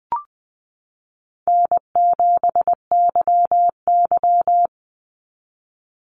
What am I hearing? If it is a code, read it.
Morse code, heard as a keyed tone, about 20 wpm, 700 Hz. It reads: N7YY
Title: Call Signs Repeated 2x - 20wpm